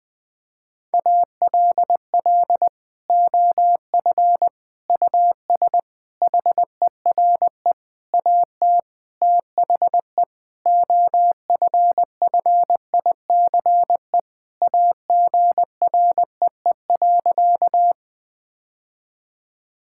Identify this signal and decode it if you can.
Morse code, heard as a keyed tone, about 21 wpm, 700 Hz. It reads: ALL OF US HERE AT THE OFFICE AGREE.
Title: Sentences with 5-Word Phrases - Spoken Beginning - 20wpm